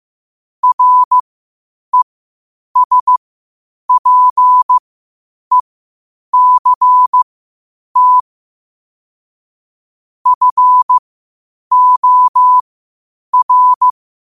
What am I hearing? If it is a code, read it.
Morse code, heard as a keyed tone, about 15 wpm, spaced as for 8 wpm, 1000 Hz. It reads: RESPECT FOR
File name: qotdfiles / 8wpm